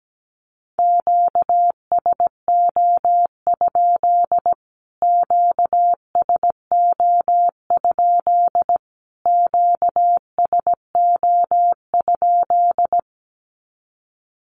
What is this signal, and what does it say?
Morse code, heard as a keyed tone, about 17 wpm, 700 Hz. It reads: QSO? QSO? QSO?
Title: Q-Codes Repeated 3x 17wpm